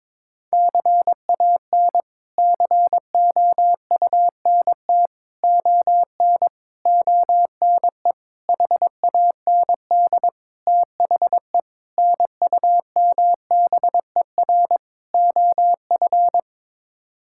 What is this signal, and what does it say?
Morse code, heard as a keyed tone, about 22 wpm, 700 Hz. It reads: CAN COUNT ON ONE HAND THE NUMBER OF
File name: practice / 22wpm / Sets of 8 Words - 22wpm